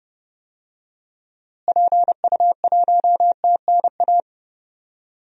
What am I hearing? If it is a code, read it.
Morse code, heard as a keyed tone, about 32 wpm, 700 Hz. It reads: PU1TNA
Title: DXCC-100 Call Signs - 6 Characters - Repeated 2x - 30wpm